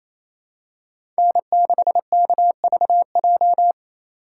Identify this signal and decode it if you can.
Morse code, heard as a keyed tone, about 28 wpm, 700 Hz. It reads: N6KVJ
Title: USA Call Signs - 3x Character Spacing - 28wpm